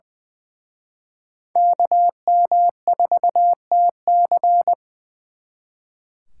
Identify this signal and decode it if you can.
Morse code, heard as a keyed tone, about 24 wpm, 700 Hz. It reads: KM4TC
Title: Speed Racing - USA 2x2 Call Signs 20wpm